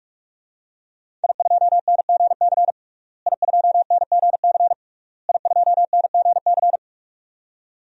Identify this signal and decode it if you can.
Morse code, heard as a keyed tone, about 46 wpm, 700 Hz. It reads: I2NGC I2NGC I2NGC
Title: DXCC-100 Call Signs - 5 Characters - Repeated 3x - 45wpm